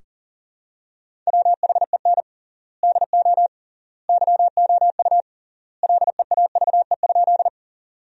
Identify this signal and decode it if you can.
Morse code, heard as a keyed tone, about 40 wpm, 700 Hz. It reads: WHEN DO YOU LEAVE?
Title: 3x Word-Spacing - Sentences from Top 400 Words - 40wpm